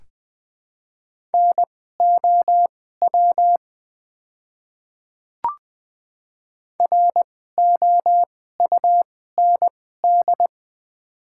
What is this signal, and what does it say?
Morse code, heard as a keyed tone, about 20 wpm, spaced as for 14 wpm, 700 Hz. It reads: NOW ROUND
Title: 2x Character Spacing - Top 200 Words - 20wpm